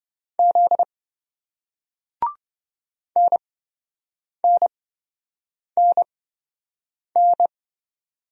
Morse code, heard as a keyed tone, about 27 wpm, spaced as for 7 wpm, 700 Hz. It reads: Z NNNN